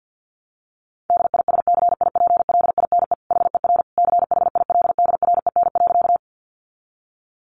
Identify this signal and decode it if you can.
Morse code, heard as a keyed tone, about 50 wpm, 700 Hz. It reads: DISCIPLINE HER CHILDREN.
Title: Sentences from Top 900 Words 50wpm